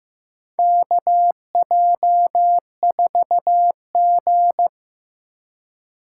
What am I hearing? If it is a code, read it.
Morse code, heard as a keyed tone, about 15 wpm, 700 Hz. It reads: KJ4G
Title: USA Call Signs - 4 Characters - 3x Character Spacing - 15wpm